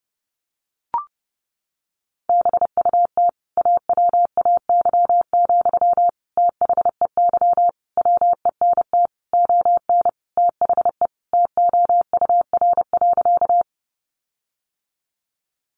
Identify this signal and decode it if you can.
Morse code, heard as a keyed tone, about 30 wpm, 700 Hz. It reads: BUT AWAY, THEY WENT ON THE TOUR.
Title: Sentences from Top 1400 Words 30wpm